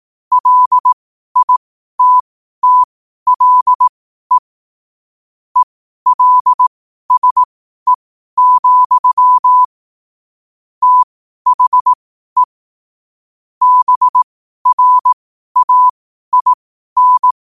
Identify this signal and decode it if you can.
Morse code, heard as a keyed tone, about 18 wpm, spaced as for 12 wpm, 1000 Hz. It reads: LITTLE ELSE, THE BRAIN
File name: qotdfiles / 12wpm